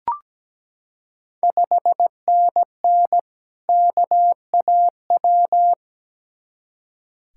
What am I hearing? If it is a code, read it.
Morse code, heard as a keyed tone, about 17 wpm, 700 Hz. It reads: 5NN KAW
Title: Ontario QSO Party Exchange 17wpm